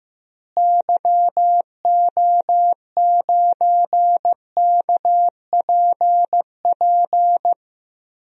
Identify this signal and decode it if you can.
Morse code, heard as a keyed tone, about 15 wpm, 700 Hz. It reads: YO9KPP
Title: DXCC-100 Call Signs - 6 Characters - Repeated 2x - 15wpm